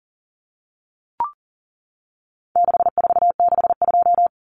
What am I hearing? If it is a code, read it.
Morse code, heard as a keyed tone, about 40 wpm, 700 Hz. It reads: 6462
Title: Numbers - 4-Digit 40wpm